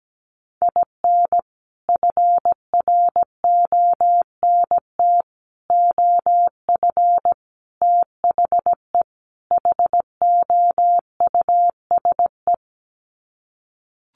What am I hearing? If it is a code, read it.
Morse code, heard as a keyed tone, about 17 wpm, 700 Hz. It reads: IN FRONT OF THE HOUSE
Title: Sets of 5 Words Encore2 17wpm